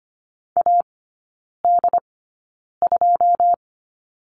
Morse code, heard as a keyed tone, about 25 wpm, spaced as for 9 wpm, 690 Hz. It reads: AD2
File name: Lesson 14e - Number 2 - Three Characters 25wpm